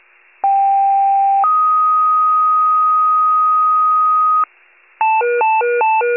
Two-tone page, heard as a one-second tone, then a three-second tone, about 790 then 1250 Hz. It is heard from 0.4 to 4.4 s.
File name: wcalert